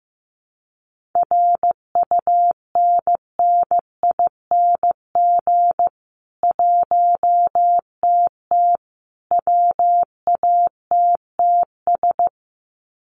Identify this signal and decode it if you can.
Morse code, heard as a keyed tone, about 15 wpm, 700 Hz. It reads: RUNNING 1TT WATTS
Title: Common QSO Phrases 15wpm